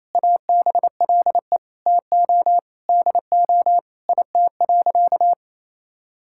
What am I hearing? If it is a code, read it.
Morse code, heard as a keyed tone, about 28 wpm, 700 Hz. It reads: ABLE TO DO IT.
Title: Sentences from Top 1000 Words - Review Part 6 - 28wpm